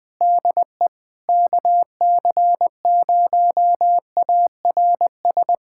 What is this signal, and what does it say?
Morse code, heard as a keyed tone, about 20 wpm, 700 Hz. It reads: DE KC0ARS
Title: USA Callsign DE Callsign - 6 Characters - 20wpm